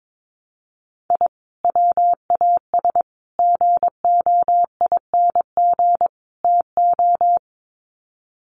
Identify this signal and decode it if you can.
Morse code, heard as a keyed tone, about 22 wpm, 700 Hz. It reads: I WAS GOING TO
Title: Sets of 4 Words 22wpm